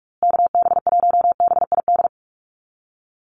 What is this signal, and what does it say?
Morse code, heard as a keyed tone, about 44 wpm, 700 Hz. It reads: KB1BID
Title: USA Call Signs Repeated 2x 45wpm